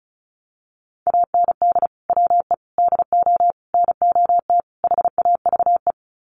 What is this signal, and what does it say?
Morse code, heard as a keyed tone, about 34 wpm, 700 Hz. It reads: AND WE DO NOT HAVE
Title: Sets of 5 Words Encore 35wpm